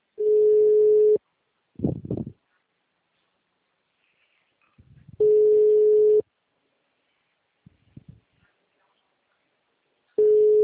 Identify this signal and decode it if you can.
Ringback tone (Europe, 425 Hz).